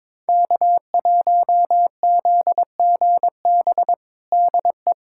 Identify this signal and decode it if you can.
Morse code, heard as a keyed tone, about 22 wpm, 700 Hz. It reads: K1ZGB DE
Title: USA Callsign DE Callsign - 5 Characters - 22wpm